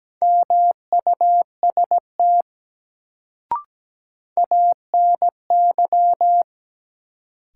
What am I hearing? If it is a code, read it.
Morse code, heard as a keyed tone, about 17 wpm, 700 Hz. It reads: MUST ANY
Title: Sentences from Top 200 Words - New Word Drill - 17wpm